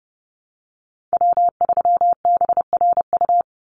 Morse code, heard as a keyed tone, about 30 wpm, 700 Hz. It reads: W3BRU
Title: USA Call Signs 30wpm